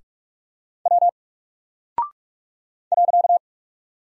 Morse code, heard as a keyed tone, about 44 wpm, 700 Hz. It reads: W .
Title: Single Character (Letter:Number:Punctuation:Prosign) Alt 45wpm